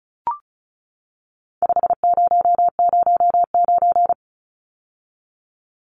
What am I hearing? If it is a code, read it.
Morse code, heard as a keyed tone, about 34 wpm, 700 Hz. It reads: 5009
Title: Numbers - 4-Digit 35wpm